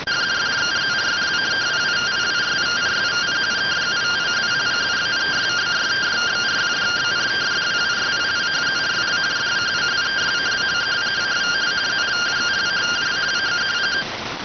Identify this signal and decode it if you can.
Two-tone teleprinter signal, about 170 Hz shift.